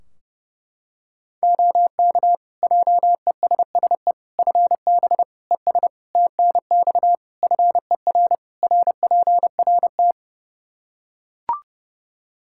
Morse code, heard as a keyed tone, about 30 wpm, 700 Hz. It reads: OK JESSE FB ES TNX FER RPRT
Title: Practice QSO - Exchange 2 - Info - 30wpm